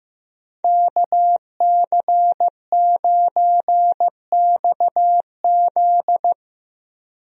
Morse code, heard as a keyed tone, about 15 wpm, 700 Hz. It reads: KC9XZ
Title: USA Call Signs - 5 Characters - Repeated 2x - 15wpm